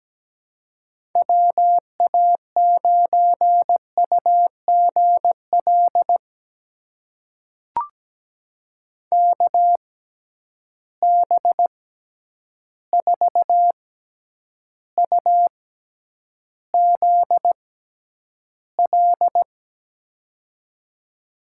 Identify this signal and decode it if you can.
Morse code, heard as a keyed tone, about 17 wpm, spaced as for 9 wpm, 700 Hz. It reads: WA9UGL KB4UZL
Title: USA Call Signs - 6x Character Spacing - 17wpm